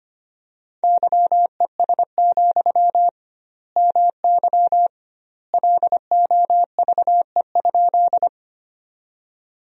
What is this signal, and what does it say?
Morse code, heard as a keyed tone, about 25 wpm, 700 Hz. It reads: YES, MY LOVE?
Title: 2x Word-Spacing - Sentences from Top 500 Words - 25wpm